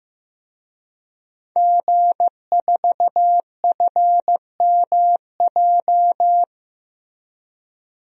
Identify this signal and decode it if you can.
Morse code, heard as a keyed tone, about 15 wpm, 700 Hz. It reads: G4FMJ
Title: DXCC-100 Call Signs - 5 Characters - Repeated 3x - 15wpm